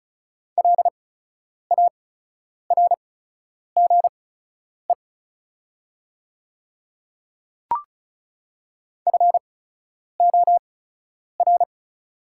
Morse code, heard as a keyed tone, about 36 wpm, spaced as for 10 wpm, 690 Hz. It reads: LARGE FOR
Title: 8x Character Spacing - Top 300 Words - 35wpm